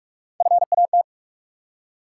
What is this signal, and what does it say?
Morse code, heard as a keyed tone, about 47 wpm, 700 Hz. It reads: FAT
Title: Three Letter Words - 45wpm